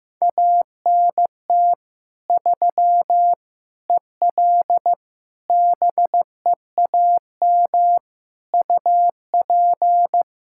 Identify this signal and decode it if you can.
Morse code, heard as a keyed tone, about 15 wpm, 700 Hz. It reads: ANT 3 EL BEAM UP
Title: Practice QSO - Exchange 2 - Info - 15wpm